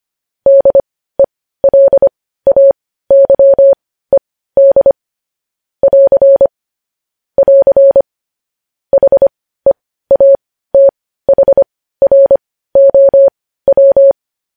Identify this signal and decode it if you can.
Morse code, heard as a keyed tone, about 25 wpm, spaced as for 15 wpm, 550 Hz. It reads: DELAYED <AR> <AR> HEATHROW
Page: news headlines in morse code at 15 wpm